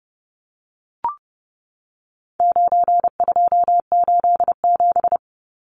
Morse code, heard as a keyed tone, about 29 wpm, 700 Hz. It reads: 9287
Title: Numbers - 4-Digit 30wpm